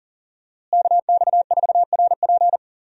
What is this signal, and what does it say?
Morse code, heard as a keyed tone, about 40 wpm, 700 Hz. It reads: KX4RP
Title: USA Call Signs - 5 Characters - Repeated 2x - 40wpm